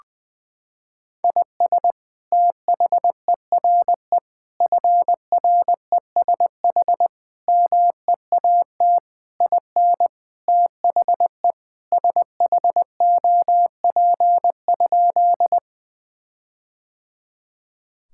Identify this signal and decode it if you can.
Morse code, heard as a keyed tone, about 20 wpm, 700 Hz. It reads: IS THERE FRESH MEAT IN THE SHOP?
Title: Sentences from Top 1600 Words 20wpm